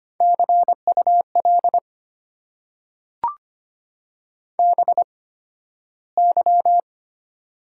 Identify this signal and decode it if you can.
Morse code, heard as a keyed tone, about 25 wpm, spaced as for 13 wpm, 700 Hz. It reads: CUL BY
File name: Top 90 QSO Elements - 8x Character Spacing - 25wpm